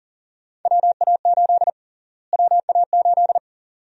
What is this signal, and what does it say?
Morse code, heard as a keyed tone, about 40 wpm, 700 Hz. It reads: WA8 WA8
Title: USA Call Sign Prefixes - Repeated 2x - 40wpm